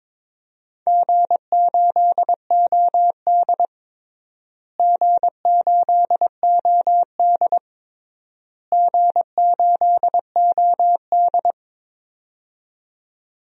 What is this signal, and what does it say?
Morse code, heard as a keyed tone, about 22 wpm, 700 Hz. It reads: G8OD G8OD G8OD
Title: DXCC-100 Call Signs - 4 Characters - Repeated 3x - 22wpm